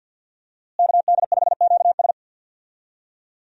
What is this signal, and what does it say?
Morse code, heard as a keyed tone, about 50 wpm, 700 Hz. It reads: KD5QS
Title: USA Call Signs - 5 Characters - Repeated 2x - 50wpm